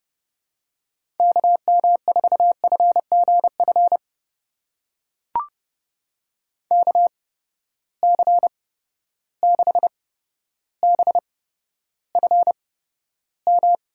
Morse code, heard as a keyed tone, about 32 wpm, spaced as for 13 wpm, 700 Hz. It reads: KM4FGF KC6BFM
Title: USA Call Signs - 6 Characters - 8x Character Spacing - 30wpm